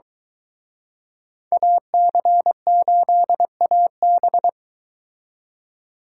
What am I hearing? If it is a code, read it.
Morse code, heard as a keyed tone, about 23 wpm, 700 Hz. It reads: AC8AB